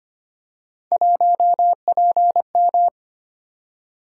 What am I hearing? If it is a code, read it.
Morse code, heard as a keyed tone, about 25 wpm, 700 Hz. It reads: 1PM
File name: Three Letter-Number Combinations - 25wpm